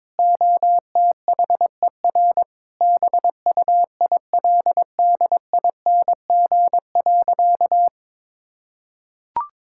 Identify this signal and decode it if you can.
Morse code, heard as a keyed tone, about 22 wpm, 700 Hz. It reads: OTHER BUILDING.
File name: practice / 22wpm / Sentences with 3-Word Phrases - 22wpm